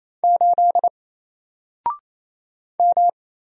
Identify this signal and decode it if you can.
Morse code, heard as a keyed tone, about 27 wpm, 700 Hz. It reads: 8 M
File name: Single-Letter-Number-28wpm